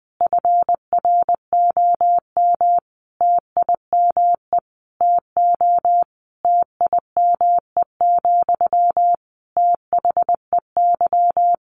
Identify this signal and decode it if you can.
Morse code, heard as a keyed tone, about 20 wpm, 700 Hz. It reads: FROM TIME TO TIME, THEY
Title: Sentences from Top 1300 Words 20wpm